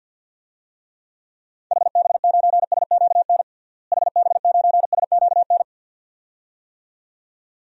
Morse code, heard as a keyed tone, about 47 wpm, 700 Hz. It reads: HB9SQN HB9SQN
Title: DXCC-100 Call Signs - 6 Characters - Repeated 2x - 50wpm